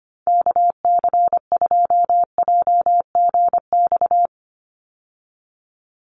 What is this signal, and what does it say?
Morse code, heard as a keyed tone, about 25 wpm, 700 Hz. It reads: KC2JGX
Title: USA Call Signs 25wpm